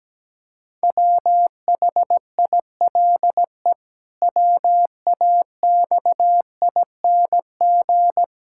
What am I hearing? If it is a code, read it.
Morse code, heard as a keyed tone, about 17 wpm, 700 Hz. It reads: WHILE WAXING
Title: Pangram Challenge v2 - 17wpm